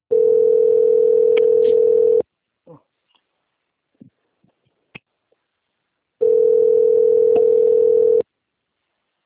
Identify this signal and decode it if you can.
Ringback tone (North America), 440 + 480 Hz, with speech.
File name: out- - - - - .35537